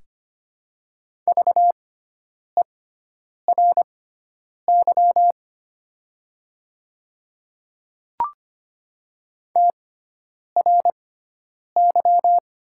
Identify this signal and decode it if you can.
Morse code, heard as a keyed tone, about 25 wpm, spaced as for 9 wpm, 700 Hz. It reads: VERY TRY
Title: 6x Character Spacing - Top 200 Words - 25wpm